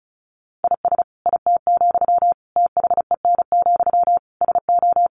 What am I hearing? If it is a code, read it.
Morse code, heard as a keyed tone, about 34 wpm, 700 Hz. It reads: IS IT, THEN, SO
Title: Sentences from Top 1500 Words 35wpm